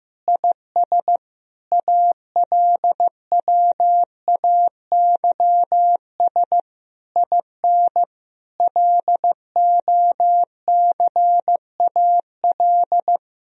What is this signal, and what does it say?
Morse code, heard as a keyed tone, about 15 wpm, 700 Hz. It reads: IS ALWAYS IN LOCAL
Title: Sentences from Top 1000 Words - Review Part 3 - 15wpm